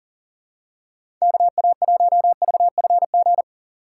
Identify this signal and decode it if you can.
Morse code, heard as a keyed tone, about 37 wpm, 700 Hz. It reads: KA1VFG
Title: USA Call Signs - 6x Character Spacing - 40wpm